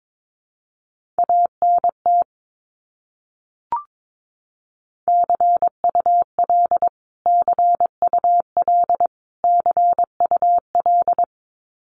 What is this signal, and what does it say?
Morse code, heard as a keyed tone, about 22 wpm, 700 Hz. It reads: ANT CUL CUL CUL
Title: Abbreviations Repeated 3x - 22wpm